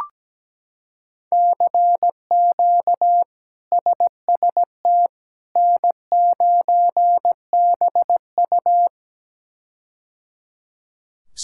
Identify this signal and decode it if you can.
Morse code, heard as a keyed tone, about 17 wpm, 700 Hz. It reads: CQ SST N9BU
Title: K1USN SST QSO Practice - 17wpm